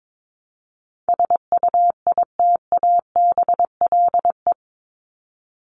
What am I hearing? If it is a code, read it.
Morse code, heard as a keyed tone, about 22 wpm, 700 Hz. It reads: SUITABLE
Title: Top 1251-1500 Words 22wpm